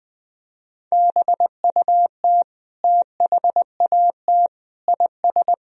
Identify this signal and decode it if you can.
Morse code, heard as a keyed tone, about 20 wpm, 700 Hz. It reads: BUT THAT IS
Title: Sentences from Top 1000 Words - Review Part 4 - 20wpm